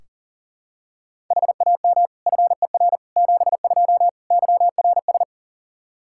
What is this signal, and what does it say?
Morse code, heard as a keyed tone, about 40 wpm, 700 Hz. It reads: HAM FER 72 YRS
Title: K1USN SST QSO Practice 40wpm